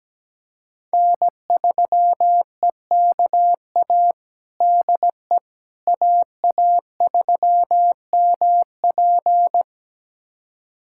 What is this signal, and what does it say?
Morse code, heard as a keyed tone, about 17 wpm, 700 Hz. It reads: N3EKA DE AA3MP
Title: USA Callsign DE Callsign - 5 Characters - 17wpm